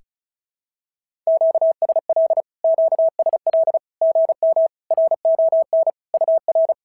Morse code, heard as a keyed tone, about 36 wpm, 690 Hz. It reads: QSL QSL GM RON UR
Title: SOTA QSO Practice 35wpm